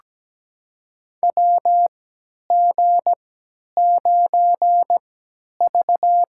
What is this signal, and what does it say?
Morse code, heard as a keyed tone, about 17 wpm, spaced as for 10 wpm, 700 Hz. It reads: WG9V
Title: USA Call Signs - 3x Character Spacing - 17wpm